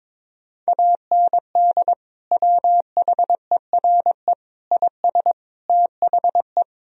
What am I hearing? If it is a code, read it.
Morse code, heard as a keyed tone, about 21 wpm, 700 Hz. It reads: AND WHERE IS THE
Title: Sentences from Top 1000 Words - Review Part 3 - 22wpm